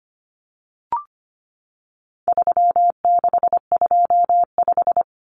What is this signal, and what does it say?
Morse code, heard as a keyed tone, about 25 wpm, 700 Hz. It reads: 3625
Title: Common Cut Numbers - 4-Digit 25wpm